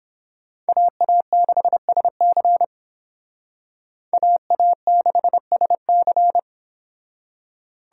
Morse code, heard as a keyed tone, about 26 wpm, 700 Hz. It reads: AA6SC AA6SC
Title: Speed Racing - USA 2x2 Call Signs 22wpm